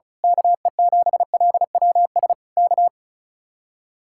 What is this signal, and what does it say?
Morse code, heard as a keyed tone, about 32 wpm, 700 Hz. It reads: KE7LWS K
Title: Speed-Racing Common QSO Phrases - 35wpm